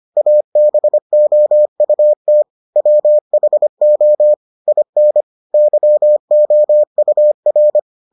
Morse code, heard as a keyed tone, about 25 wpm, 600 Hz. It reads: ABOUT WHO IN YOUR